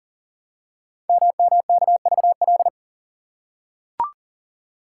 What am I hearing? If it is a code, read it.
Morse code, heard as a keyed tone, about 42 wpm, 690 Hz. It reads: MMKVL
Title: Code Groups - 5 Letters - 40wpm